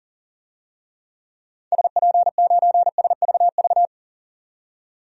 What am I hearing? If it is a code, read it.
Morse code, heard as a keyed tone, about 40 wpm, 700 Hz. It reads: SP9SVV
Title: DXCC-100 Call Signs - 6 Characters - 40wpm